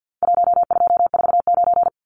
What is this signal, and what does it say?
Morse code, heard as a keyed tone, about 48 wpm, 700 Hz. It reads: 1249
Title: Numbers - 4-Digit 50wpm